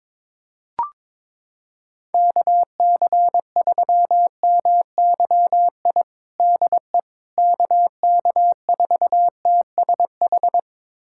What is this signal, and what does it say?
Morse code, heard as a keyed tone, about 22 wpm, 700 Hz. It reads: KC3MYI DE KK4TSH
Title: USA Callsign DE Callsign - 6 Characters - 22wpm